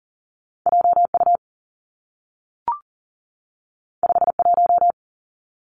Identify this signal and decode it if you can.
Morse code, heard as a keyed tone, about 40 wpm, 700 Hz. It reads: JU 51
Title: Two Challenging Letter-Number Combinations 40wpm